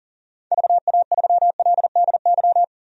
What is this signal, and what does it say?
Morse code, heard as a keyed tone, about 40 wpm, 700 Hz. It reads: VA3LDY
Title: DXCC-100 Call Signs - 6 Characters - 40wpm